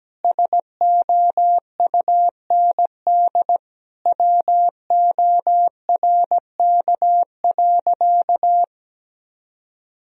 Morse code, heard as a keyed tone, about 17 wpm, 700 Hz. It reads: SOUND WORK.
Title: Sentences from Top 1000 Words - Review Part 2 - 17wpm